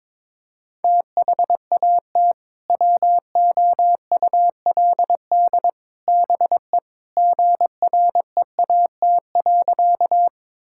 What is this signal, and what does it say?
Morse code, heard as a keyed tone, about 21 wpm, 700 Hz. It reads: THAT WOULD BE GREAT.
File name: practice / 22wpm / Sentences with 3-Word Phrases - 22wpm